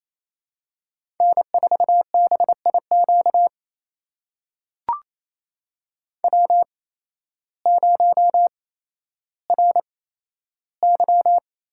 Morse code, heard as a keyed tone, about 28 wpm, spaced as for 13 wpm, 700 Hz. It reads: N4BIQ W0RY
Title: USA Call Signs - 8x Character Spacing - 28wpm